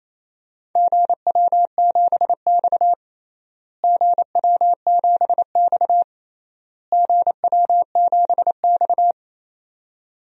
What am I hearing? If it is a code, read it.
Morse code, heard as a keyed tone, about 28 wpm, 700 Hz. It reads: GW7X GW7X GW7X
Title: DXCC-100 Call Signs - 4 Characters - Repeated 3x - 28wpm